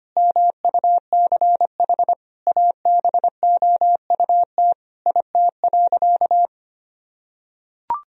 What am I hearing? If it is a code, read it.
Morse code, heard as a keyed tone, about 25 wpm, 700 Hz. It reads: MUCH ABOUT IT.
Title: Sentences from Top 1000 Words - Review Part 2 - 25wpm